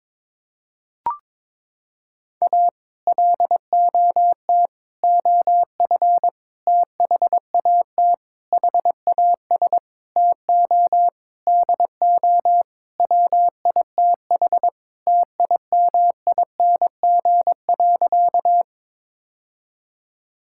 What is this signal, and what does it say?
Morse code, heard as a keyed tone, about 22 wpm, 700 Hz. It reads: A LOT OF THAT HAS TO DO WITH TIMING.
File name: practice / 22wpm / Sentences with 8-Word Phrases - 22wpm